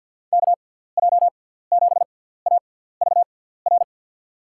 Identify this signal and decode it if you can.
Morse code, heard as a keyed tone, about 52 wpm, spaced as for 17 wpm, 700 Hz. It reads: KJ7AVR
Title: USA Call Signs - 6 Characters - 6x Character Spacing - 50wpm